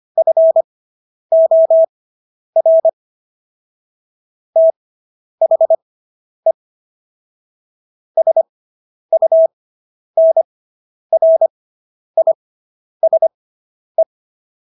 Morse code, heard as a keyed tone, about 25 wpm, spaced as for 10 wpm, 650 Hz. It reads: FOR THE SUNRISE